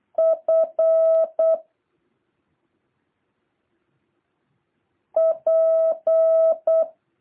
Morse code, heard as a keyed tone, about 8 wpm, 650 Hz. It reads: F P
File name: jh8cbh / 2023cw